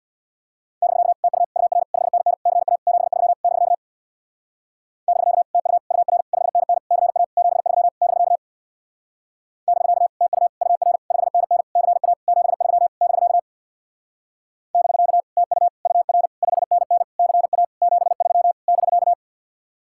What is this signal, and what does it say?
Morse code, heard as a keyed tone, about 58 wpm, 700 Hz. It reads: <BK> TU UR 5NN CA 73 <BK> <BK> TU UR 5NN CA 73 <BK> <BK> TU UR 5NN CA 73 <BK> <BK> TU UR 5NN CA 73 <BK>
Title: Speed-Racing Common QSO Phrases Encore - 50wpm